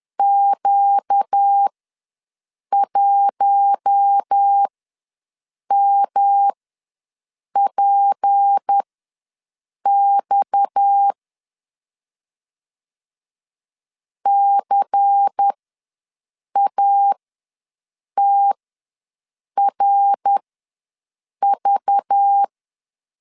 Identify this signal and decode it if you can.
Morse code, heard as a keyed tone, about 10 wpm, spaced as for 6 wpm, 800 Hz. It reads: Q1MPX CATRV
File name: LEZIONI / 6 PKX0 / 6 PKX ws 10wpm 3706